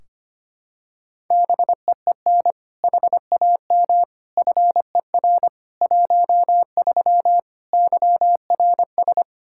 Morse code, heard as a keyed tone, about 25 wpm, 700 Hz. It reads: BEEN HAM FER 13 YRS
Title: Practice QSO - Exchange 3 - More Info - 25wpm